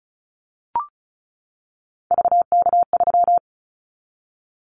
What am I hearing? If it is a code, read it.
Morse code, heard as a keyed tone, about 36 wpm, 700 Hz. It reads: VK3
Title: Callsign Prefixes for Top 100 DXCC - 35wpm